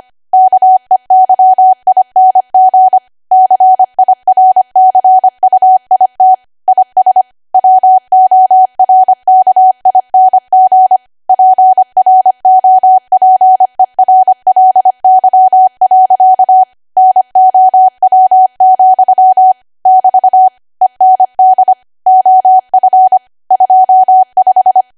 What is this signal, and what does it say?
Morse code, heard as a keyed tone, about 25 wpm, 750 Hz. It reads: KEYING CIRCUIT IS WORKING PROPERLY. NOW, <BT> END OF 25
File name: Archive / 25 WPM / 25WPM